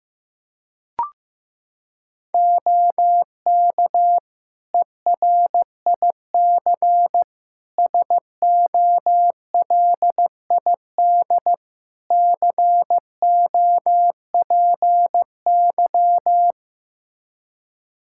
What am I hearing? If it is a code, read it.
Morse code, heard as a keyed tone, about 16 wpm, 700 Hz. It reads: OK ERIC SOLID COPY
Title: Practice QSO - Exchange 3 - More Info - 15wpm